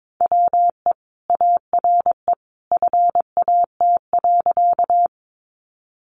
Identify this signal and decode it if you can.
Morse code, heard as a keyed tone, about 22 wpm, 700 Hz. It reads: WE ARE FAT.
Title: Sentences from Top 600 Words 22wpm